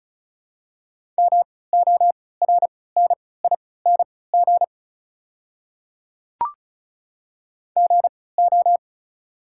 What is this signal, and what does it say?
Morse code, heard as a keyed tone, about 36 wpm, spaced as for 20 wpm, 690 Hz. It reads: MORNING GO
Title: 3x Character Spacing - Top 500 Words - 35wpm